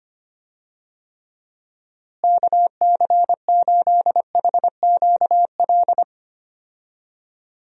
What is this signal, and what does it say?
Morse code, heard as a keyed tone, about 25 wpm, 700 Hz. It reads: KC8HQL
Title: USA Call Signs - 6 Characters - Repeated 2x - 25wpm